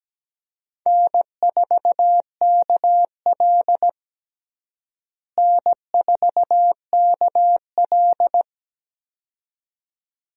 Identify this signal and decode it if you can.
Morse code, heard as a keyed tone, about 17 wpm, 700 Hz. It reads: N4KL N4KL
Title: USA Call Signs - 4 Characters - Repeated 2x - 17wpm